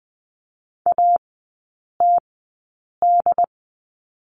Morse code, heard as a keyed tone, about 20 wpm, spaced as for 8 wpm, 700 Hz. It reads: ATD
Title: 067 Lesson 12e - Letter d - Three Characters 20wpm